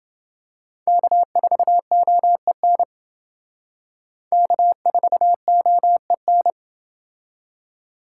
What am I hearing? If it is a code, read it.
Morse code, heard as a keyed tone, about 29 wpm, 700 Hz. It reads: K4OEN K4OEN